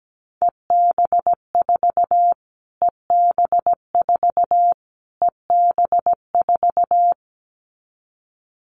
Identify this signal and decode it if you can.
Morse code, heard as a keyed tone, about 17 wpm, 700 Hz. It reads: EB4 EB4 EB4
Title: Callsign Prefixes for Top 25 DXCC - Repeated 3x - 17wpm